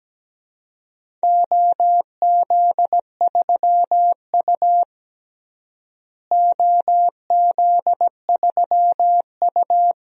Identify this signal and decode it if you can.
Morse code, heard as a keyed tone, about 17 wpm, 700 Hz. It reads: OZ3U OZ3U
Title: DXCC-100 Call Signs - 4 Characters - Repeated 3x - 17wpm